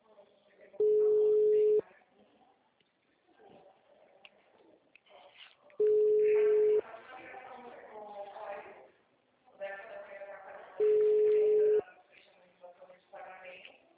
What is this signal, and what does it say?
Ringback tone (Europe, 425 Hz), with speech.
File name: out- -2011- - - .10748